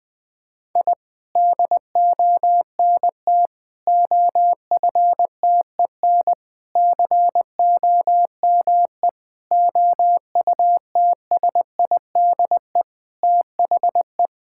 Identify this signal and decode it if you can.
Morse code, heard as a keyed tone, about 21 wpm, 700 Hz. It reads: I DONT OFTEN COME OUTSIDE THE
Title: Sentences from Top 1000 Words - Review Part 6 - 20wpm